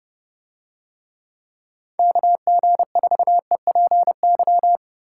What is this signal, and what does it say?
Morse code, heard as a keyed tone, about 29 wpm, 700 Hz. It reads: KG4EPY